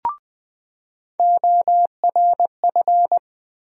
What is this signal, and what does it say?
Morse code, heard as a keyed tone, about 20 wpm, 700 Hz. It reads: ORF
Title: USA Call Sign Postfixes - 3 Letters 20wpm